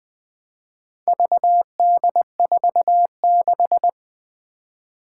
Morse code, heard as a keyed tone, about 20 wpm, 700 Hz. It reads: VD46
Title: Four Challenging Letter-Number Combinations 20wpm